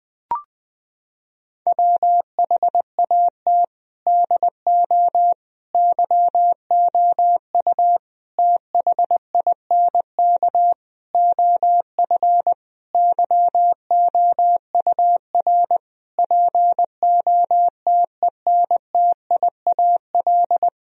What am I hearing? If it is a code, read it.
Morse code, heard as a keyed tone, about 20 wpm, 700 Hz. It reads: WHAT DO YOU THINK OF YOUR POTENTIAL
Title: Sentences from Top 1000 Words - Review Part 4 - 20wpm